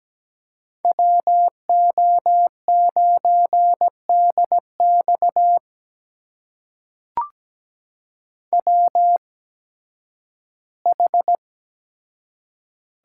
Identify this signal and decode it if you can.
Morse code, heard as a keyed tone, about 17 wpm, spaced as for 11 wpm, 700 Hz. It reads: WO9DX WH
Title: USA Call Signs - 5 Characters - 8x Character Spacing - 17wpm